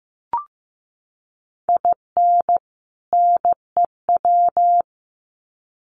Morse code, heard as a keyed tone, about 15 wpm, 700 Hz. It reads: IN NEW
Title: Sets of 2 Words 15wpm